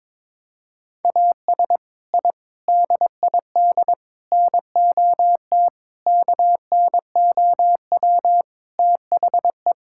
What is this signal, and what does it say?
Morse code, heard as a keyed tone, about 22 wpm, 700 Hz. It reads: AS I DID NOT KNOW THE
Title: Sentences from Top 1000 Words - Review Part 6 - 22wpm